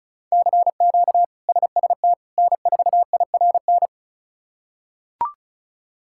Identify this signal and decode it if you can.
Morse code, heard as a keyed tone, about 34 wpm, 700 Hz. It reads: CQ SST N4IRN
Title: K1USN SST QSO Practice - 35wpm